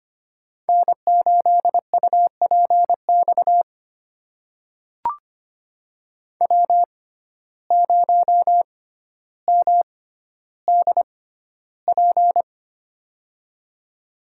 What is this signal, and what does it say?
Morse code, heard as a keyed tone, about 25 wpm, spaced as for 13 wpm, 700 Hz. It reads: N8UPX W0MDP
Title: USA Call Signs - 5 Characters - 6x Character Spacing - 25wpm